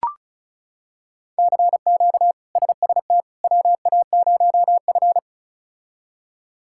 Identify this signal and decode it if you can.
Morse code, heard as a keyed tone, about 34 wpm, 700 Hz. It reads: CQ SST WA0F
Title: K1USN SST QSO Practice - 35wpm